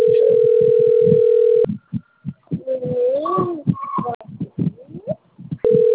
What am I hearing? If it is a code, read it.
Ringback tone (North America), 440 + 480 Hz.